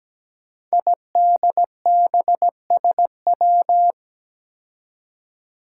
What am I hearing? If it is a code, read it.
Morse code, heard as a keyed tone, about 17 wpm, 700 Hz. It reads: IDBSW
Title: Five Challenging Letter-Number Combinations 17wpm